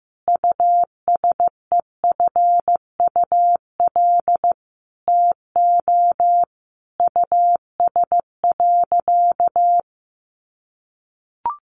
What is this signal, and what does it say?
Morse code, heard as a keyed tone, about 15 wpm, 700 Hz. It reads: USEFUL TO US.
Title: Sentences from Top 700 Words 15wpm